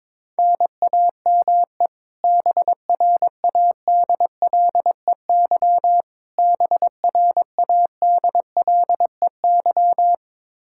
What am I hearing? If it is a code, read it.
Morse code, heard as a keyed tone, about 21 wpm, 700 Hz. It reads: NAME BRADLEY BRADLEY
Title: Practice QSO - Exchange 1 - Beginning - 22wpm